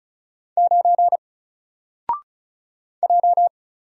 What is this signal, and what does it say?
Morse code, heard as a keyed tone, about 36 wpm, 700 Hz. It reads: 9 J
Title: Single Character (Letter:Number:Punctuation:Prosign) 35wpm